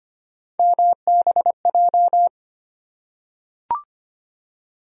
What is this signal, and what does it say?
Morse code, heard as a keyed tone, about 25 wpm, 700 Hz. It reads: MBJ
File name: Three Letter-Number Combinations - 25wpm